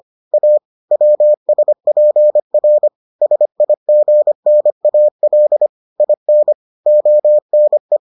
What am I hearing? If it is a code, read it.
Morse code, heard as a keyed tone, about 25 wpm, 600 Hz. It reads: A WSPR SIGNAL IN ONE